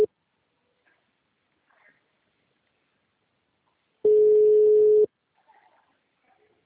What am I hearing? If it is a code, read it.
Ringback tone (Europe, 425 Hz).
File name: out- -2004- - - .14